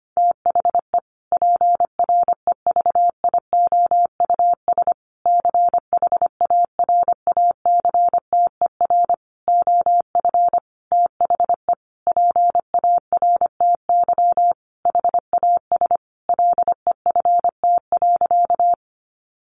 Morse code, heard as a keyed tone, about 25 wpm, 700 Hz. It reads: THE PREVIOUS CHARACTER OF THE PARTY HAS LEFT.
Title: Sentences from Top 1500 Words 25wpm